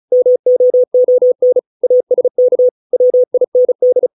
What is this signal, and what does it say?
Morse code, heard as a keyed tone, about 34 wpm, 500 Hz. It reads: MOON ASK WIND